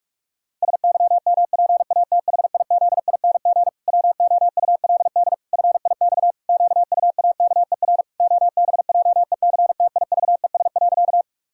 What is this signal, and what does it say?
Morse code, heard as a keyed tone, about 46 wpm, 700 Hz. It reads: SYMPATHIZING WOULD FIX QUAKER OBJECTIVES.
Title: Pangram Challenge v2 - 45wpm